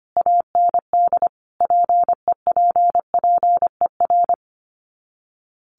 Morse code, heard as a keyed tone, about 25 wpm, 700 Hz. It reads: AND PEPPER
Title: Sets of 5 Words 25wpm